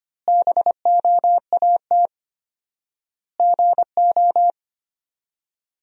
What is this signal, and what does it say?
Morse code, heard as a keyed tone, about 25 wpm, 700 Hz. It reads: BOAT GO
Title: 4x Word-Spacing - Sentences from Top 500 Words - 25wpm